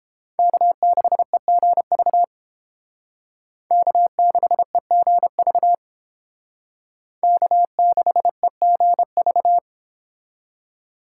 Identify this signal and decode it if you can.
Morse code, heard as a keyed tone, about 29 wpm, 700 Hz. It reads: K6EGV K6EGV K6EGV